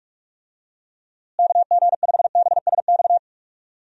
Morse code, heard as a keyed tone, about 41 wpm, 700 Hz. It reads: KG5BSX
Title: USA Call Signs - 6 Characters - Repeated 3x - 45wpm